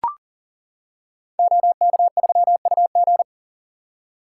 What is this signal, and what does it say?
Morse code, heard as a keyed tone, about 39 wpm, 700 Hz. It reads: OK3UG
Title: Code Groups - 5 Letter:Numbers - 40wpm